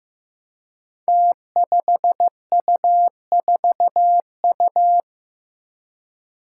Morse code, heard as a keyed tone, about 15 wpm, 700 Hz. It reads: T5U4U
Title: Five Challenging Letter-Number Combinations 15wpm